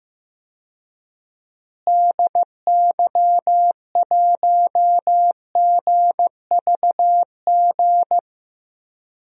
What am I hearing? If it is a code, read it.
Morse code, heard as a keyed tone, about 16 wpm, 700 Hz. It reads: DY1GVG
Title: DXCC-100 Call Signs - 6 Characters - Repeated 2x - 15wpm